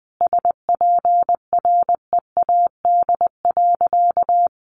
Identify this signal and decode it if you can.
Morse code, heard as a keyed tone, about 20 wpm, 700 Hz. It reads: SPREAD.
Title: Sentences from Top 1200 Words 20wpm